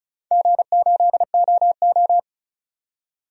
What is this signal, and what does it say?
Morse code, heard as a keyed tone, about 36 wpm, 700 Hz. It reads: G8OO
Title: DXCC-100 Call Signs - 4 Characters - 35wpm